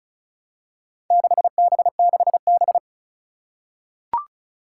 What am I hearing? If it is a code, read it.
Morse code, heard as a keyed tone, about 36 wpm, 700 Hz. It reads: 6B6B
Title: Four Challenging Letter-Number Combinations 35wpm